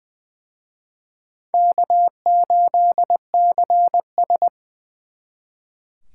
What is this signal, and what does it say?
Morse code, heard as a keyed tone, about 24 wpm, 700 Hz. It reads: K8CS
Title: Speed Racing - USA 1x2 Call Signs 20wpm